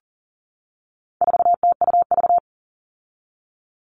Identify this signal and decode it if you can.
Morse code, heard as a keyed tone, about 40 wpm, 700 Hz. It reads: 4TUV
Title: All Cut Numbers - 4-Digit 40wpm